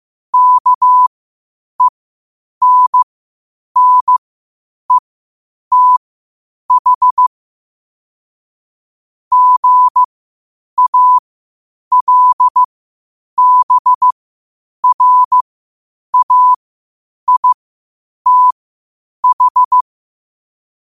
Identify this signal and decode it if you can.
Morse code, heard as a keyed tone, about 15 wpm, spaced as for 8 wpm, 1000 Hz. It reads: KENNETH GALBRAITH
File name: qotdfiles / 8wpm